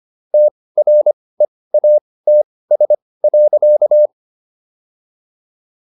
Morse code, heard as a keyed tone, about 25 wpm, spaced as for 18 wpm, 600 Hz. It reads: TREATS.